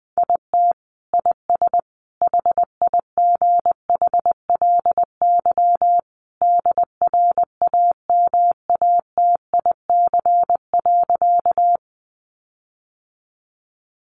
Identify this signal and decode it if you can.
Morse code, heard as a keyed tone, about 20 wpm, 700 Hz. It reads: IT IS HIGHLY DRAMATIC.
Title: Sentences from Top 1300 Words 20wpm